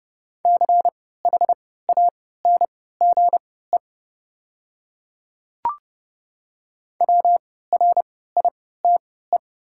Morse code, heard as a keyed tone, about 29 wpm, spaced as for 17 wpm, 700 Hz. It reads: CHANGE WRITE
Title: 3x Character Spacing - Top 200 Words - 30wpm